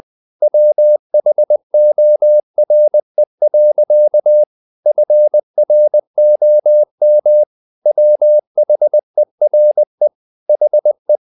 Morse code, heard as a keyed tone, about 20 wpm, 600 Hz. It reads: WHORE. FROM WHERE HE
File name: the old man and the sea / the old man and the sea part 1 20wpm